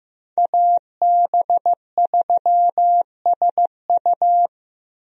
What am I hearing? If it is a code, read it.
Morse code, heard as a keyed tone, about 15 wpm, 700 Hz. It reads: AB3SU
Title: USA Call Signs - 5 Characters - Repeated 3x - 15wpm